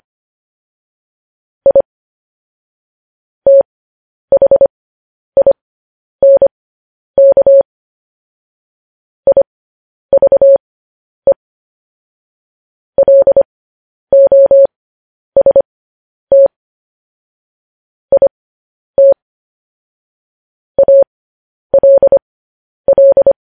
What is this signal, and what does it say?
Morse code, heard as a keyed tone, about 25 wpm, spaced as for 10 wpm, 550 Hz. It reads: I THINK IVE LOST IT ALL